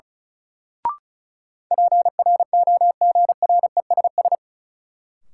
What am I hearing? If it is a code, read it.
Morse code, heard as a keyed tone, about 36 wpm, 700 Hz. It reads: PROGRESS
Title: Sentences from Top 1400 Words - New Word Drill - 35wpm